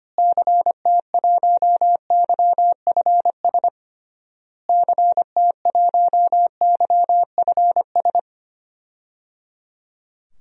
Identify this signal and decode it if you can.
Morse code, heard as a keyed tone, about 25 wpm, 700 Hz. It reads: CT1YFS CT1YFS
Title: DXCC-100 Call Signs - 6 Characters - Repeated 3x - 25wpm